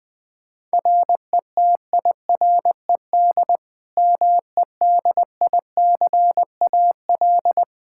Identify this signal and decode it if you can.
Morse code, heard as a keyed tone, about 21 wpm, 700 Hz. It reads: RETIRED MEDICAL
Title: Practice QSO - Exchange 3 - More Info - 20wpm